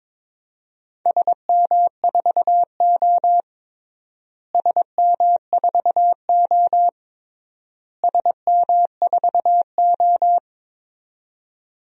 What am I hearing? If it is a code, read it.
Morse code, heard as a keyed tone, about 22 wpm, 700 Hz. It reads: SM4O SM4O SM4O
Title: DXCC-100 Call Signs - 4 Characters - Repeated 3x - 22wpm